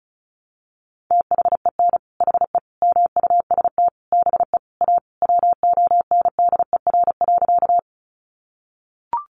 Morse code, heard as a keyed tone, about 34 wpm, 700 Hz. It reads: THEN HE MUST BE A WONDER.
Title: Sentences from Top 500 Words 35wpm